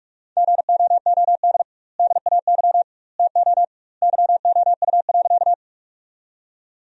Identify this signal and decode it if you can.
Morse code, heard as a keyed tone, about 42 wpm, 700 Hz. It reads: GOOD DAY TO YOU.
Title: 2x Word-Spacing - Sentences from Top 200 Words - 45wpm